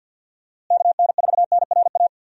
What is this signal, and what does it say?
Morse code, heard as a keyed tone, about 47 wpm, 700 Hz. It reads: KN4NRA
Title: USA Call Signs - 6 Characters - Repeated 3x - 50wpm